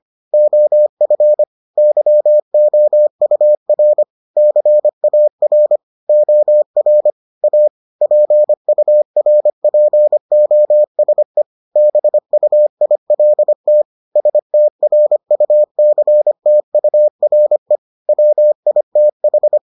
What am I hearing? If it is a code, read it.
Morse code, heard as a keyed tone, about 25 wpm, 600 Hz. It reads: OF YOUR CAR OR A PURPOSE BUILT STRUCTURE WITH